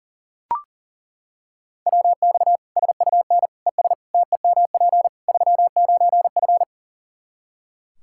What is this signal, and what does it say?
Morse code, heard as a keyed tone, about 40 wpm, 700 Hz. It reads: WX SUN ES TEMP 39F
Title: Practice QSO - Exchange 2 - Info - 40wpm